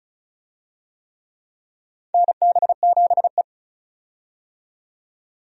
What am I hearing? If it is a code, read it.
Morse code, heard as a keyed tone, about 36 wpm, 700 Hz. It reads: NB7E
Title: USA Call Signs - 4 Characters - 35wpm